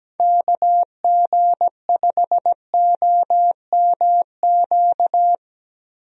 Morse code, heard as a keyed tone, about 17 wpm, 700 Hz. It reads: KG5OMQ
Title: USA Call Signs - 6 Characters - Repeated 2x - 17wpm